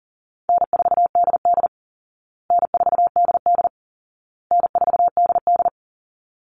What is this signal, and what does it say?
Morse code, heard as a keyed tone, about 40 wpm, 700 Hz. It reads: N4DD N4DD N4DD
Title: Easy Call Signs Repeated 3x - 40wpm